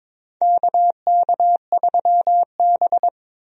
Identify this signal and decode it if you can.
Morse code, heard as a keyed tone, about 22 wpm, 700 Hz. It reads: KK3B
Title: USA Call Signs - 4 Characters - 22wpm